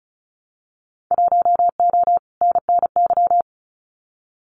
Morse code, heard as a keyed tone, about 36 wpm, 700 Hz. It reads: 1O NNY
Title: Winter Field Day Exchanges Encore 35wpm